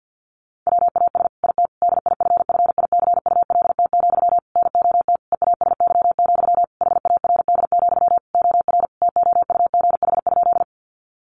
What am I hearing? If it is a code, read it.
Morse code, heard as a keyed tone, about 50 wpm, 700 Hz. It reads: WAS IT DIFFICULT, NOT EASY, HARD, OR TOUGH?
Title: Sentences from Top 1200 Words 50wpm